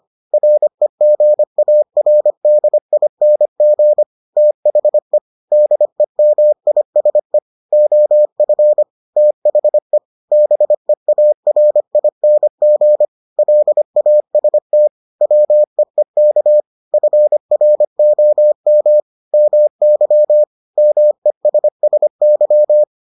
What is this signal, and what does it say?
Morse code, heard as a keyed tone, about 25 wpm, 600 Hz. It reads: REGARDING THE DEMISE OF THE BEARING LAST WEEK FROM MY MESSY